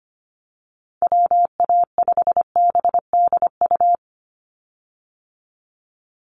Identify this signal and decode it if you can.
Morse code, heard as a keyed tone, about 25 wpm, 700 Hz. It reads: WA5BDU
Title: Call Signs Repeated 3x - 25wpm